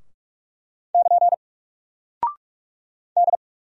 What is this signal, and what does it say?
Morse code, heard as a keyed tone, about 46 wpm, 700 Hz. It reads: <KN> D
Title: Single Character (Letter:Number:Punctuation:Prosign) 45wpm